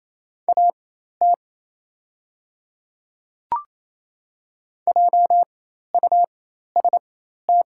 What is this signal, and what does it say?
Morse code, heard as a keyed tone, about 28 wpm, spaced as for 13 wpm, 700 Hz. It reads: AT JUST
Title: 4x Character Spacing - Top 200 Words - 28wpm